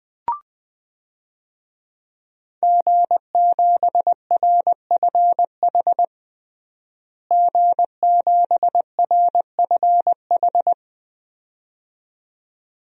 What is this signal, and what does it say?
Morse code, heard as a keyed tone, about 20 wpm, 700 Hz. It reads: G7RFH G7RFH
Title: DXCC-100 Call Signs - 5 Characters - Repeated 2x - 20wpm